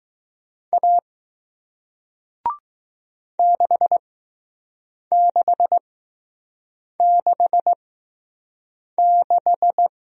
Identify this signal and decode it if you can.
Morse code, heard as a keyed tone, about 20 wpm, 700 Hz. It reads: A 6 6 6 6